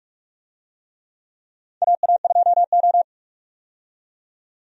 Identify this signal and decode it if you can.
Morse code, heard as a keyed tone, about 46 wpm, 700 Hz. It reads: AA2O
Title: USA Call Signs - 4 Characters - 45wpm